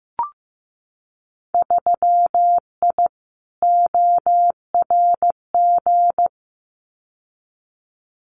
Morse code, heard as a keyed tone, about 15 wpm, 700 Hz. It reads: 3I ORG
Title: Winter Field Day Exchanges Encore 15wpm